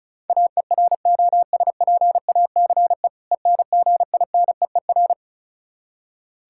Morse code, heard as a keyed tone, about 36 wpm, 700 Hz. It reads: AEROSPACE ENGINEER
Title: Practice QSO - Exchange 3 - More Info - 35wpm